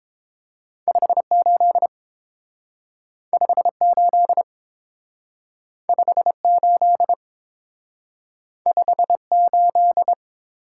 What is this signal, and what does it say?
Morse code, heard as a keyed tone, about 30 wpm, 700 Hz. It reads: 58 58 58 58